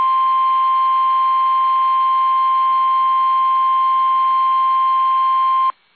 Emergency Alert System (SAME) tones (the 1050 Hz attention tone). It is heard from 0.5 to 5.2 s.
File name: eas / KUSO / KUSO